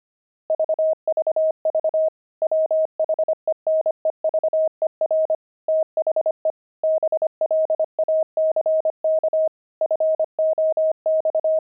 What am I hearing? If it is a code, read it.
Morse code, heard as a keyed tone, about 25 wpm, 630 Hz. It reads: VVV WHENEVER THE BLACK FOX
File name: pangram-fs-25-cs-25-squirrel